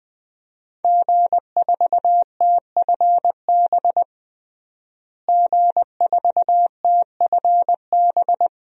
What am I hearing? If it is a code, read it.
Morse code, heard as a keyed tone, about 20 wpm, 700 Hz. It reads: G4TFB G4TFB
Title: DXCC-100 Call Signs - 5 Characters - Repeated 3x - 20wpm